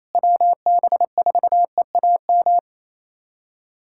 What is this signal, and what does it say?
Morse code, heard as a keyed tone, about 28 wpm, 700 Hz. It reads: WB4EAM
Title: USA Callsign DE Callsign - 6 Characters - 28wpm